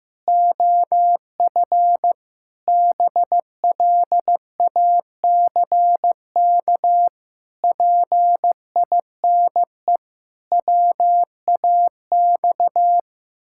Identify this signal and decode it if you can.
Morse code, heard as a keyed tone, about 16 wpm, 700 Hz. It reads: OF BLACK PINE WAX
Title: Pangram Challenge v2 - 15wpm